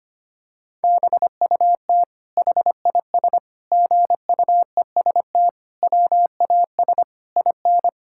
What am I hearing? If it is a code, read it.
Morse code, heard as a keyed tone, about 25 wpm, 700 Hz. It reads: BUT HIS GUEST WAS IN
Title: Sentences from Top 1600 Words 25wpm